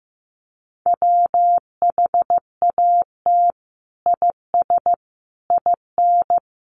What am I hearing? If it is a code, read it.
Morse code, heard as a keyed tone, about 15 wpm, 700 Hz. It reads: WHAT IS IN
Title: Sentences from Top 200 Words 15wpm